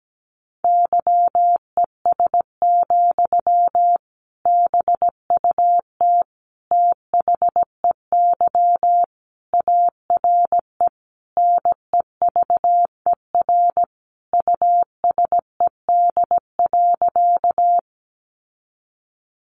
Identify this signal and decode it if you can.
Morse code, heard as a keyed tone, about 17 wpm, 700 Hz. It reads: YES, BUT THEY ARE NEVER USED.
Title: Sentences from Top 600 Words 17wpm